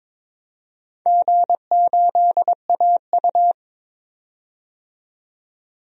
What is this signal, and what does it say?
Morse code, heard as a keyed tone, about 22 wpm, 700 Hz. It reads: G8AU
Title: DXCC-100 Call Signs - 4 Characters - 22wpm